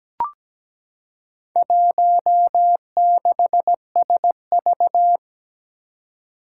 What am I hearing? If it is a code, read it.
Morse code, heard as a keyed tone, about 17 wpm, 700 Hz. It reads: 16SV
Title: Four Challenging Letter-Number Combinations 17wpm